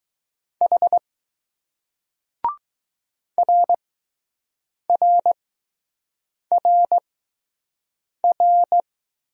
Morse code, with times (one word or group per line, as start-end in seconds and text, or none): 0.6-1.0 H
3.4-8.8 RRRR